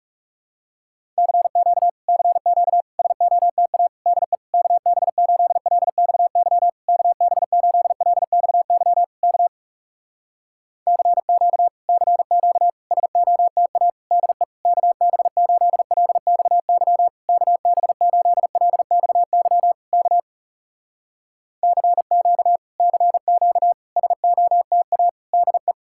Morse code, heard as a keyed tone, about 38 wpm, 700 Hz. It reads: CQ CQ SOTA DE KB8LXY KB8LXY K CQ CQ SOTA DE KB8LXY KB8LXY K CQ CQ SOTA DE